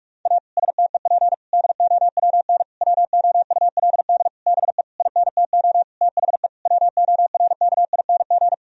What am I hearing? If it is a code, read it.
Morse code, heard as a keyed tone, about 46 wpm, 700 Hz. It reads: A STEP DOWN WOULD BE INTO THE WORKING
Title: Sentences from Top 1000 Words - Review Part 4 - 45wpm